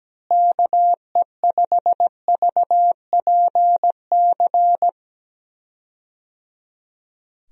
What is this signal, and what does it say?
Morse code, heard as a keyed tone, about 17 wpm, 700 Hz. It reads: KE5VPC
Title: USA Call Signs - 6 Characters - Repeated 3x - 17wpm